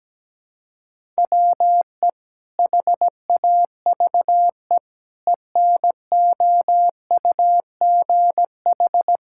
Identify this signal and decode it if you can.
Morse code, heard as a keyed tone, about 17 wpm, 700 Hz. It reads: WE HAVE ENOUGH
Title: Sentences from Top 1000 Words - Review Part 5 - 17wpm